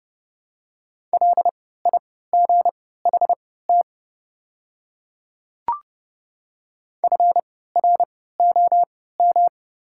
Morse code, heard as a keyed tone, about 30 wpm, spaced as for 17 wpm, 690 Hz. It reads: LIGHT FROM
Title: 3x Character Spacing - Top 200 Words - 30wpm